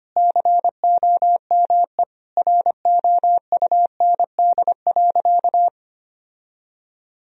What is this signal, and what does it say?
Morse code, heard as a keyed tone, about 25 wpm, 700 Hz. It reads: COME ROUND.
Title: Sentences from Top 1000 Words - Review Part 1 - 25wpm